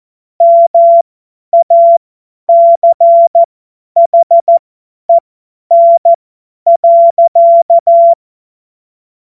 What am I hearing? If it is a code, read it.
Morse code, heard as a keyed tone, about 14 wpm, spaced as for 10 wpm, 680 Hz. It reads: MACHEN.
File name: Morsekurs / Witze / Witz 05. Oktober 2021, 50 F70